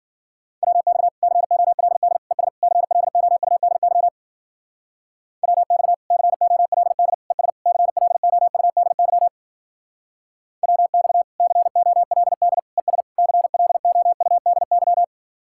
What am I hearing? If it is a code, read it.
Morse code, heard as a keyed tone, about 50 wpm, 700 Hz. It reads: WX COLD ES CLOUDY WX COLD ES CLOUDY WX COLD ES CLOUDY
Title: Speed-Racing Common QSO Phrases - 40wpm